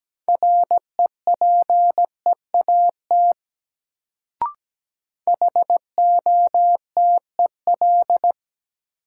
Morse code, heard as a keyed tone, about 17 wpm, 700 Hz. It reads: REPEAT HOTEL
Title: Sentences from Top 1600 Words - New Word Drill - 17wpm